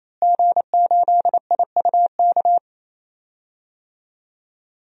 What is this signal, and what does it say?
Morse code, heard as a keyed tone, about 28 wpm, 700 Hz. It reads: G8IUK
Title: DXCC-100 Call Signs - 5 Characters - Repeated 3x - 28wpm